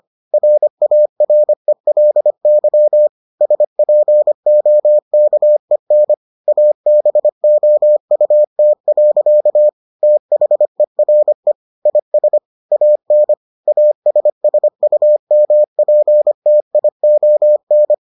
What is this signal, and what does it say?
Morse code, heard as a keyed tone, about 25 wpm, 600 Hz. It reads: RARELY SPOKEN ABOUT. THERE IS AN ASSUMPTION